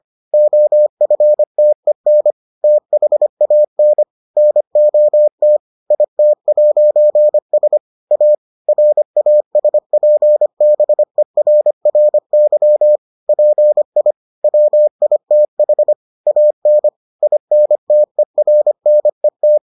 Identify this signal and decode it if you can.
Morse code, heard as a keyed tone, about 25 wpm, 600 Hz. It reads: OFTEN THAN NOT IT'S A RASPBERRY PI WITH AN INTERNET